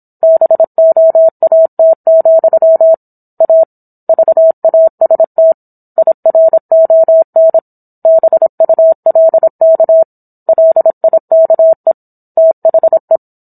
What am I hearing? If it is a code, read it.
Morse code, heard as a keyed tone, about 27 wpm, 650 Hz. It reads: BOAT, A VAST IRON BULK LIKE THE